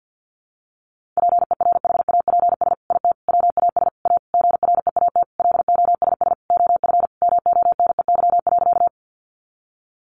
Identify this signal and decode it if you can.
Morse code, heard as a keyed tone, about 50 wpm, 700 Hz. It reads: PERHAPS IT WAS A GREAT LOSS OF MONEY.
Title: Sentences from Top 1400 Words 50wpm